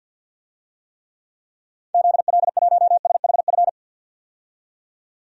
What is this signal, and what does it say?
Morse code, heard as a keyed tone, about 49 wpm, 700 Hz. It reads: ZL1SHF
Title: DXCC-100 Call Signs - 6 Characters - 50wpm